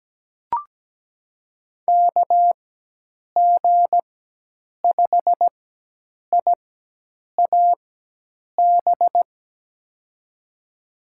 Morse code, heard as a keyed tone, about 17 wpm, spaced as for 8 wpm, 700 Hz. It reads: KG5IAB
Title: USA Call Signs - 6 Characters - 4x Character Spacing - 17wpm